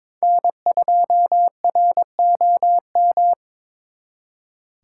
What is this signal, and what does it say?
Morse code, heard as a keyed tone, about 26 wpm, 700 Hz. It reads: N2ROM
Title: Speed Racing - USA 1x3 Call Signs 22wpm